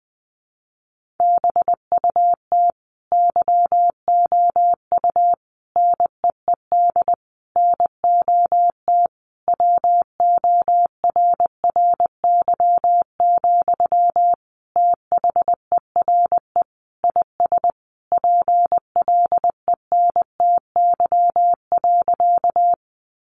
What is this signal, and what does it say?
Morse code, heard as a keyed tone, about 20 wpm, 700 Hz. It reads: BUT YOU NEED NOT WORRY, THERE IS PLENTY.
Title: Sentences from Top 1300 Words 20wpm